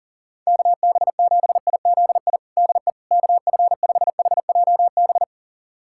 Morse code, heard as a keyed tone, about 40 wpm, 700 Hz. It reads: KB7IZI DE KF5HJB
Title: USA Callsign DE Callsign - 6 Characters - 40wpm